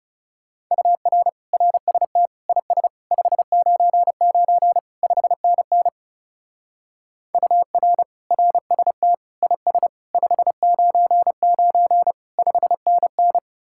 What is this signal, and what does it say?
Morse code, heard as a keyed tone, about 36 wpm, 700 Hz. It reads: UR RST IS 599 5NN UR RST IS 599 5NN
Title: Speed-Racing Common QSO Phrases - 30wpm